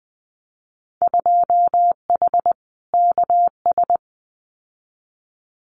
Morse code, heard as a keyed tone, about 20 wpm, 700 Hz. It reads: 2H KS
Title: Winter Field Day Exchanges Encore 20wpm